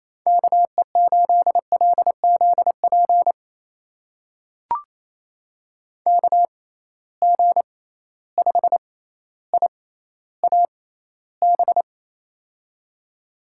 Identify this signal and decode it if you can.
Morse code, heard as a keyed tone, about 28 wpm, spaced as for 14 wpm, 700 Hz. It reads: KE8LZP KG5IAB
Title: USA Call Signs - 6 Characters - 6x Character Spacing - 28wpm